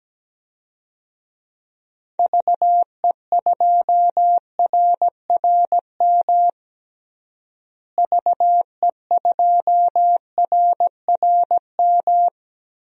Morse code, heard as a keyed tone, about 17 wpm, 700 Hz. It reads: VE2RRM VE2RRM
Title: DXCC-100 Call Signs - 6 Characters - Repeated 2x - 17wpm